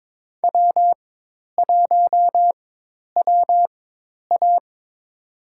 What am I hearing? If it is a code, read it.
Morse code, heard as a keyed tone, about 22 wpm, spaced as for 10 wpm, 700 Hz. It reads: W1WA
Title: USA Call Signs - 4 Characters - 4x Character Spacing - 22wpm